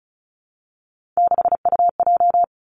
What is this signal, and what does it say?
Morse code, heard as a keyed tone, about 36 wpm, 700 Hz. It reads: 6UJ
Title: Three Challenging Letter-Number Combinations 35wpm